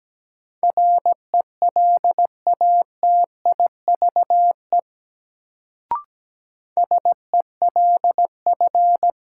Morse code, heard as a keyed tone, about 17 wpm, 700 Hz. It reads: RELATIVE SELF
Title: Sentences from Top 1400 Words - New Word Drill Alt - 17wpm